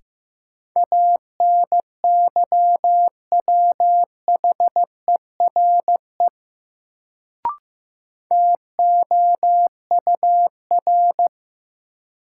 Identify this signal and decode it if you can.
Morse code, heard as a keyed tone, about 15 wpm, 700 Hz. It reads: ANYWHERE TOUR
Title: Sentences from Top 1400 Words - New Word Drill - 15wpm